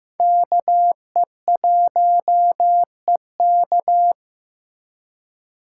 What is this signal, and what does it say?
Morse code, heard as a keyed tone, about 15 wpm, 700 Hz. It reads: KE1EK
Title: USA Call Signs - 5 Characters - 2x Character Spacing - 15wpm